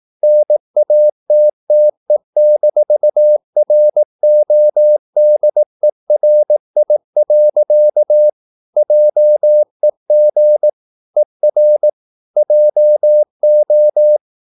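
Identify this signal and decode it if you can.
Morse code, heard as a keyed tone, about 18 wpm, 600 Hz. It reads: NATTE-RODERI. JEG ER JO
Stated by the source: Slow (QRS) Morse practice at 18 wpm, CW med Gnister.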